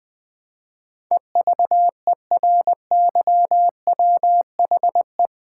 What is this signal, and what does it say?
Morse code, heard as a keyed tone, about 20 wpm, 700 Hz. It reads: EVERYWHE
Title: Build A Word - 4 Letters At A Time - 10-Letter Words - 20wpm